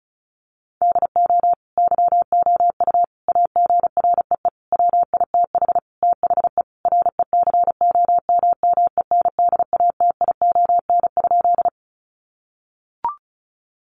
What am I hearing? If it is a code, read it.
Morse code, heard as a keyed tone, about 34 wpm, 700 Hz. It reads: DO YOU AGREE WITH THE RECOMMENDATION?
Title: Sentences from Top 1400 Words 35wpm